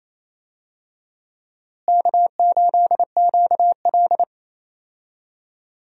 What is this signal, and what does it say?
Morse code, heard as a keyed tone, about 28 wpm, 700 Hz. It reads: K8QL
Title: USA Call Signs - 4 Characters - Repeated 3x - 28wpm